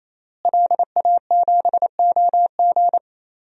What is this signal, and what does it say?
Morse code, heard as a keyed tone, about 28 wpm, 700 Hz. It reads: LA7OG
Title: DXCC-100 Call Signs - 5 Characters - Repeated 3x - 28wpm